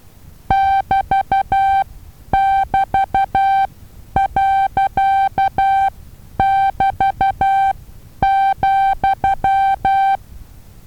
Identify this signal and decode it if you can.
Morse code, heard as a keyed tone, about 12 wpm, spaced as for 9 wpm, 780 Hz. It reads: <BT><BT>.<BT>,
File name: Taviro HA6VA B